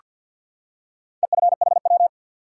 Morse code, heard as a keyed tone, about 48 wpm, 700 Hz. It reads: EFHW
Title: Top 180 QSO Elements 50wpm